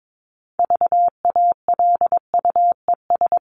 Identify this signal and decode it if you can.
Morse code, heard as a keyed tone, about 22 wpm, 700 Hz. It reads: VALUES
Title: ES:S Suffix - Top 100 Words with Base Words 22wpm